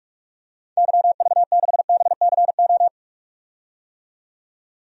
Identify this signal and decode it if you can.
Morse code, heard as a keyed tone, about 46 wpm, 700 Hz. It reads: YV6BCO
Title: DXCC-100 Call Signs - 6 Characters - 45wpm